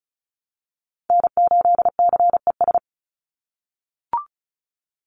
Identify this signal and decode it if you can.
Morse code, heard as a keyed tone, about 36 wpm, 700 Hz. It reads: N8CES
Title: USA Call Signs 35wpm